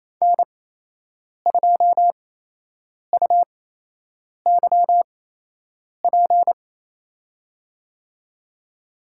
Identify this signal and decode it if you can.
Morse code, heard as a keyed tone, about 28 wpm, spaced as for 8 wpm, 700 Hz. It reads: N2UYP
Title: USA Call Signs - 8x Character Spacing - 28wpm